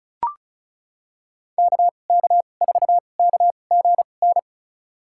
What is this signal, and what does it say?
Morse code, heard as a keyed tone, about 36 wpm, spaced as for 25 wpm, 700 Hz. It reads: KK4KGN
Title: USA Call Signs - 6 Characters - 2x Character Spacing - 35wpm